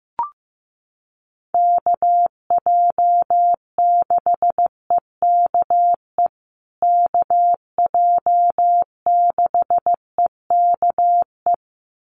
Morse code, heard as a keyed tone, about 15 wpm, 700 Hz. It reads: KJ6EKE KJ6EKE
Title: USA Call Signs Repeated 2x 15wpm